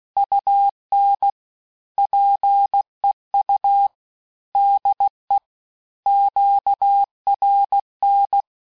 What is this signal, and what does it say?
Morse code, heard as a keyed tone, about 16 wpm, 800 Hz. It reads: UN PEU DE QRN
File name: cours-du-lundi files / cours-1580-fin-QSO